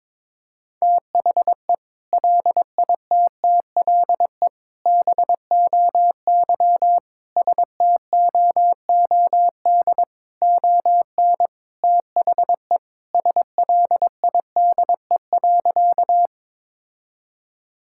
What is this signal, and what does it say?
Morse code, heard as a keyed tone, about 21 wpm, 700 Hz. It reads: THE LITTLE BOY STOOD ON THE SLIDE.
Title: Sentences from Top 1600 Words 22wpm